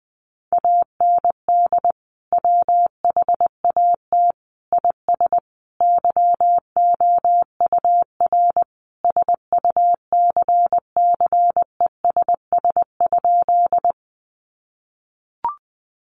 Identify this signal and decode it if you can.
Morse code, heard as a keyed tone, about 20 wpm, 700 Hz. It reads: AND WHAT IS YOUR SUCCESS?
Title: Sentences from Top 900 Words 20wpm